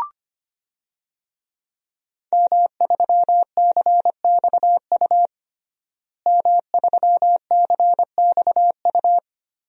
Morse code, heard as a keyed tone, about 25 wpm, 700 Hz. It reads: M3CXU M3CXU
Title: DXCC-100 Call Signs - 5 Characters - Repeated 3x - 25wpm